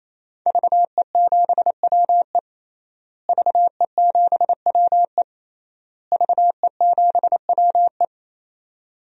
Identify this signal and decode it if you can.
Morse code, heard as a keyed tone, about 28 wpm, 700 Hz. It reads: VE7WE VE7WE VE7WE
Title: DXCC-100 Call Signs - 5 Characters - Repeated 3x - 28wpm